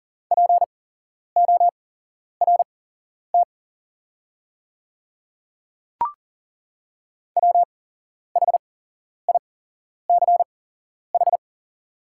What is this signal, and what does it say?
Morse code, heard as a keyed tone, about 40 wpm, spaced as for 11 wpm, 690 Hz. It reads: PORT WHICH
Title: 8x Character Spacing - Top 300 Words - 40wpm